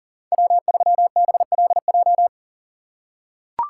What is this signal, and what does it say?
Morse code, heard as a keyed tone, about 40 wpm, 700 Hz. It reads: W3BLJ
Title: USA Call Signs - 5 Characters - 4x Character Spacing - 40wpm